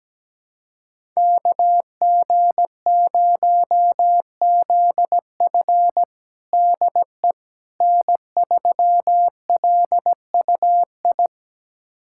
Morse code, heard as a keyed tone, about 17 wpm, 700 Hz. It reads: KG0ZF DE N3LUI
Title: USA Callsign DE Callsign - 5 Characters - 17wpm